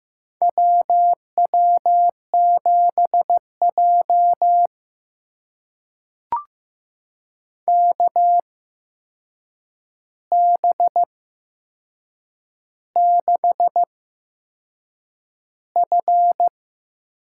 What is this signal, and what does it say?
Morse code, heard as a keyed tone, about 15 wpm, spaced as for 6 wpm, 700 Hz. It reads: WW7J KB6F
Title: USA Call Signs - 8x Character Spacing - 15wpm